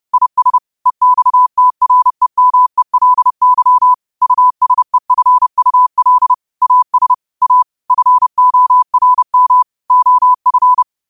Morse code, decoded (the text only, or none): IS EXTREMELY USEFUL AS A FORM OF